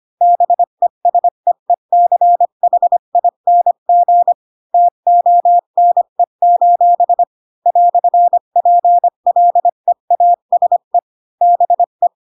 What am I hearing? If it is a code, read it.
Morse code, heard as a keyed tone, about 25 wpm, 700 Hz. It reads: BESEECHING TONE: "PLEASE BE
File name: Pinocchio / Pinocchio ch1 20 25 ik1hge